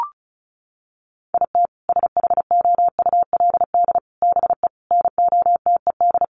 Morse code, heard as a keyed tone, about 34 wpm, 700 Hz. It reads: IT SHOULD BE NOTED
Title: Sets of 5 Words 35wpm